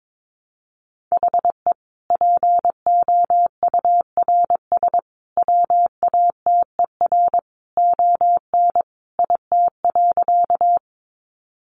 Morse code, heard as a keyed tone, about 22 wpm, 700 Hz. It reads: HE POURS WATER ON IT.
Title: Additional Easy Phrases 22wpm